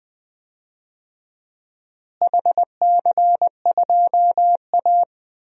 Morse code, heard as a keyed tone, about 20 wpm, 700 Hz. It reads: HC2A
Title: DXCC-100 Call Signs - 4 Characters - Repeated 3x - 20wpm